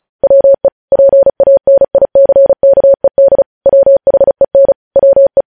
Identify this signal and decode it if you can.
Morse code, heard as a keyed tone, about 36 wpm, 550 Hz. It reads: WE PANICKED WHEN WE
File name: podcast / News headlines-1x-35x35